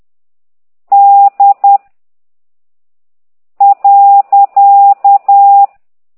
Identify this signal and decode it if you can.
Morse code, heard as a keyed tone, about 10 wpm, 800 Hz. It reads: D .
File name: exerc7